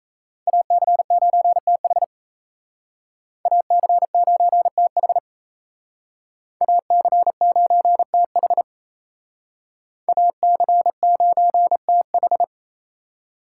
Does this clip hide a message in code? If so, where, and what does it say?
Yes: Morse code from 0.5 to 12.4 s, heard as a keyed tone, about 37 wpm, 700 Hz. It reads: AC9TH AC9TH AC9TH AC9TH